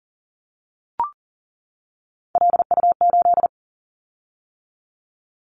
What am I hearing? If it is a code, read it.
Morse code, heard as a keyed tone, about 40 wpm, 700 Hz. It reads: LU8
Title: Callsign Prefixes for Top 100 DXCC - 40wpm